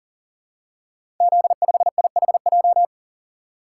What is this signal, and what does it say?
Morse code, heard as a keyed tone, about 39 wpm, 700 Hz. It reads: Z5IHJ
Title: Code Groups - 5 Letter:Numbers - 40wpm